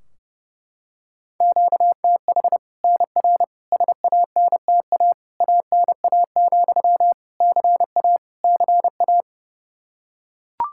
Morse code, heard as a keyed tone, about 32 wpm, 700 Hz. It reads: QTH NR SANTA ANA, CA CA
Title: Practice QSO - Exchange 1 - Beginning - 30wpm